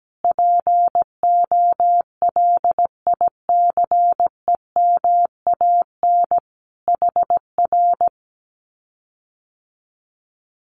Morse code, heard as a keyed tone, about 17 wpm, 700 Hz. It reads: POLICEMAN HR
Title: Common QSO Phrases Encore 17wpm